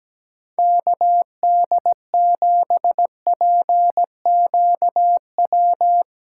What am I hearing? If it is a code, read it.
Morse code, heard as a keyed tone, about 17 wpm, 700 Hz. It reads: KD7PQW
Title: USA Call Signs - 6x Character Spacing - 17wpm